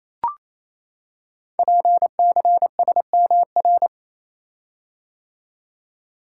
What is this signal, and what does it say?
Morse code, heard as a keyed tone, about 27 wpm, 700 Hz. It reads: PCSMR
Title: Code Groups - 5 Letter:Numbers - 28wpm